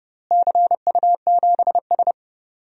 Morse code, heard as a keyed tone, about 29 wpm, 700 Hz. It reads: CU7S
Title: DXCC-100 Call Signs - 4 Characters - Repeated 2x - 30wpm